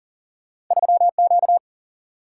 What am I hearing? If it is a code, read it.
Morse code, heard as a keyed tone, about 40 wpm, 700 Hz. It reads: 3 Q
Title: Single-Letter Single-Number 40wpm